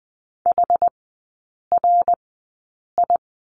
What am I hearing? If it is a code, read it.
Morse code, heard as a keyed tone, about 20 wpm, spaced as for 8 wpm, 700 Hz. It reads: HRI
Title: 060 Lesson 11e - Letter h - Three Characters 20wpm